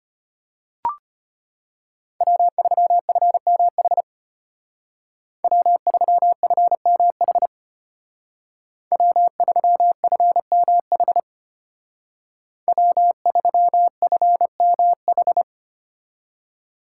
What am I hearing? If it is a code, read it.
Morse code, heard as a keyed tone, about 33 wpm, 700 Hz. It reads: W3FMH W3FMH W3FMH W3FMH